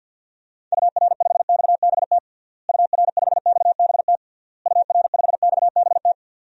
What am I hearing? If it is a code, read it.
Morse code, heard as a keyed tone, about 52 wpm, 700 Hz. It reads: UR5XBT UR5XBT UR5XBT
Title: DXCC-100 Call Signs - 6 Characters - Repeated 3x - 50wpm